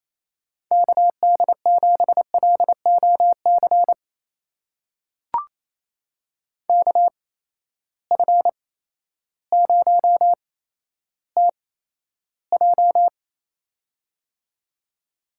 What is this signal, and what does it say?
Morse code, heard as a keyed tone, about 28 wpm, spaced as for 13 wpm, 700 Hz. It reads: KD7LOC KF0TJ
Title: USA Call Signs - 8x Character Spacing - 28wpm